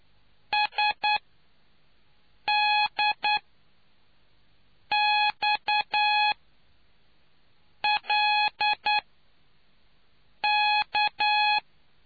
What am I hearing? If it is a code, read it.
Morse code, heard as a keyed tone, about 9 wpm, spaced as for 5 wpm, 840 Hz. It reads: SDXLK